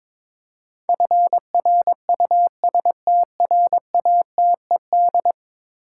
Morse code, heard as a keyed tone, about 22 wpm, 700 Hz. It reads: FRUSTRATED
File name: practice / 22wpm / 10 Letter Words - First 3 Letters Given - 22wpm